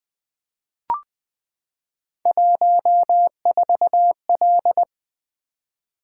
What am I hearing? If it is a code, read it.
Morse code, heard as a keyed tone, about 20 wpm, 700 Hz. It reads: 14L
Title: Three Letter-Number Combinations - 20wpm